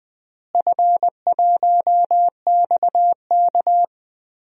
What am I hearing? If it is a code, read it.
Morse code, heard as a keyed tone, about 21 wpm, 700 Hz. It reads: F1XK
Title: DXCC-100 Call Signs - 4 Characters - Repeated 2x - 20wpm